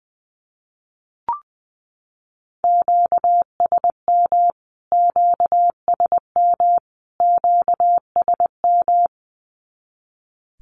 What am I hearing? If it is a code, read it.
Morse code, heard as a keyed tone, about 20 wpm, 700 Hz. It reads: QSM QSM QSM
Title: Q-Codes Repeated 3x 20wpm